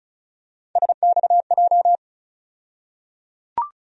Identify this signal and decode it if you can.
Morse code, heard as a keyed tone, about 36 wpm, 700 Hz. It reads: SXJ
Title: USA Call Sign Postfixes - 3 Letters 35wpm